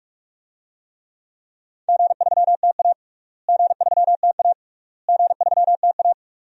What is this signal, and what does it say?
Morse code, heard as a keyed tone, about 46 wpm, 700 Hz. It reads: G3TA G3TA G3TA
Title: DXCC-100 Call Signs - 4 Characters - Repeated 3x - 45wpm